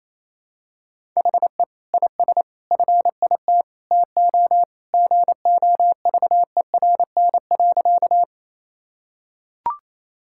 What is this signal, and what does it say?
Morse code, heard as a keyed tone, about 28 wpm, 700 Hz. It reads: HE IS FIT TO GOVERN.
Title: Sentences from Top 1000 Words - Review Part 5 - 28wpm